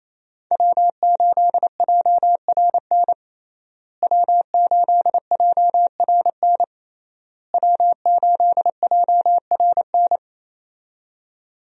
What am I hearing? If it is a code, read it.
Morse code, heard as a keyed tone, about 28 wpm, 700 Hz. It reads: W8JRN W8JRN W8JRN
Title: USA Call Signs - 5 Characters - Repeated 3x - 28wpm